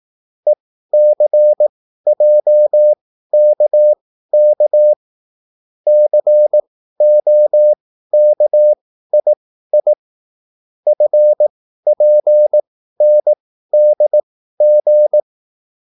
Morse code, heard as a keyed tone, about 18 wpm, spaced as for 13 wpm, 600 Hz. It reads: ECJKK COKII FPNDG